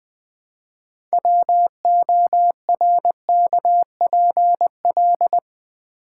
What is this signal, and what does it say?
Morse code, heard as a keyed tone, about 20 wpm, 700 Hz. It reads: WORKPL
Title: Build A Word - 3 Letters At A Time - 9-Letter Words - 20wpm